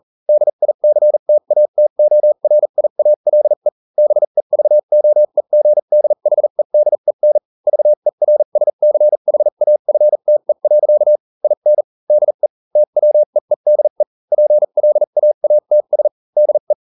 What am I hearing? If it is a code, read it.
Morse code, heard as a keyed tone, about 40 wpm, spaced as for 35 wpm, 600 Hz. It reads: DICTATORIALE BEVOEGDHEDEN VERSCHAFTE. IN DE TWEEDE PLAATS DE